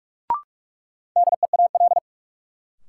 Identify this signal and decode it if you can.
Morse code, heard as a keyed tone, about 41 wpm, 700 Hz. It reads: DEAL
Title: Sentences from Top 800 Words - New Word Drill - 45wpm